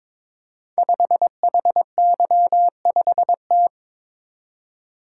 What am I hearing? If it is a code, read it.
Morse code, heard as a keyed tone, about 22 wpm, 700 Hz. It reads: 5HY5T
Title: Code Groups - 5 Letter:Numbers - 22wpm